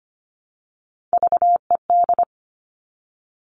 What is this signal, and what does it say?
Morse code, heard as a keyed tone, about 25 wpm, 700 Hz. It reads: VED
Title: All Cut Numbers - 3-Digit 25wpm